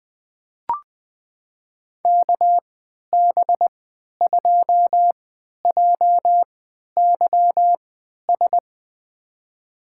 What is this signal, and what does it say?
Morse code, heard as a keyed tone, about 20 wpm, spaced as for 11 wpm, 700 Hz. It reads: KB2JYS
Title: USA Call Signs - 6 Characters - 3x Character Spacing - 20wpm